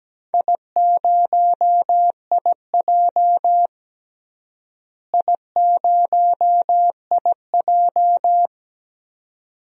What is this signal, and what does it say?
Morse code, heard as a keyed tone, about 17 wpm, 700 Hz. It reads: I0IJ I0IJ
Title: DXCC-100 Call Signs - 4 Characters - Repeated 2x - 17wpm